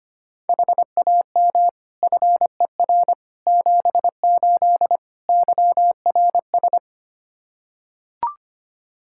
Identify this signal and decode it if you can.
Morse code, heard as a keyed tone, about 25 wpm, 700 Hz. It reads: HAM FER 78 YRS
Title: Practice QSO - Exchange 3 - More Info - 25wpm